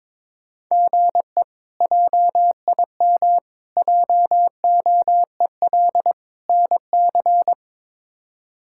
Morse code, heard as a keyed tone, about 22 wpm, 700 Hz. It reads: GE JIM JOEL NC
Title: K1USN SST QSO Practice - 22wpm